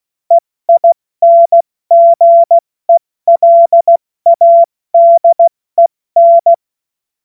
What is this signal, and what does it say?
Morse code, heard as a keyed tone, about 16 wpm, 680 Hz. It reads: EINGELADEN
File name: Morsekurs / Witze / Witz 08. Februar 2022 70 F80